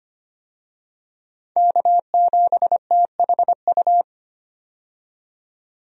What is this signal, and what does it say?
Morse code, heard as a keyed tone, about 25 wpm, 700 Hz. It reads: K7THU
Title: USA Call Signs - 5 Characters - 25wpm